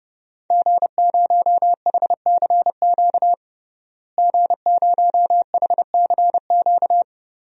Morse code, heard as a keyed tone, about 32 wpm, 700 Hz. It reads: G0HCQ G0HCQ
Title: DXCC-100 Call Signs - 5 Characters - Repeated 3x - 30wpm